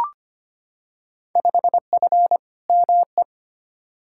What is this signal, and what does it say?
Morse code, heard as a keyed tone, about 25 wpm, 700 Hz. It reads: 5F ME
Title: Field Day Exchanges Encore - 25wpm